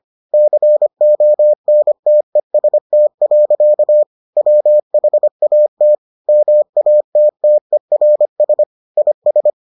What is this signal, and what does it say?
Morse code, heard as a keyed tone, about 25 wpm, 600 Hz. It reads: CONTEST. WHAT MATTERS IS